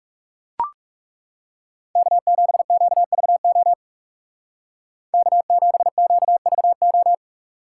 Morse code, heard as a keyed tone, about 40 wpm, 700 Hz. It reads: K7QVO K7QVO